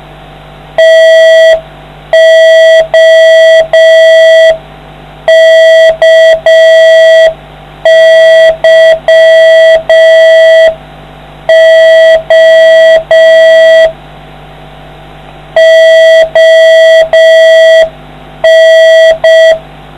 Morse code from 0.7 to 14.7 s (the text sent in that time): TOKYO